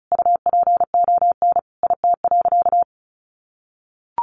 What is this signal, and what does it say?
Morse code, heard as a keyed tone, about 34 wpm, 700 Hz. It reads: UPON IT.
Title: Sentences from Top 1400 Words 35wpm